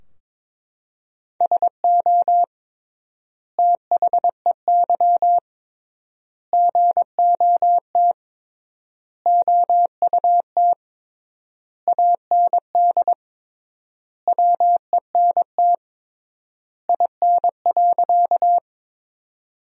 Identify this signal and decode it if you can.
Morse code, heard as a keyed tone, about 22 wpm, 700 Hz. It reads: SO THEY GOT OUT AND WENT IN.
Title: 3x Word-Spacing - Sentences from Top 300 Words - 22wpm